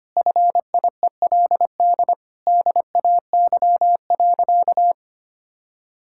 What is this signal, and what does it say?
Morse code, heard as a keyed tone, about 25 wpm, 700 Hz. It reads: FIELD DAY.
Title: Sentences from Top 1000 Words - Review Part 6 - 25wpm